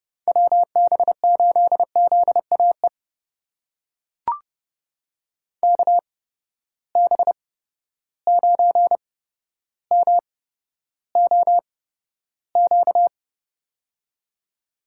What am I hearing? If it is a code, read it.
Morse code, heard as a keyed tone, about 30 wpm, spaced as for 13 wpm, 700 Hz. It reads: WB8ZAE KB9MOQ
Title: USA Call Signs - 6 Characters - 8x Character Spacing - 30wpm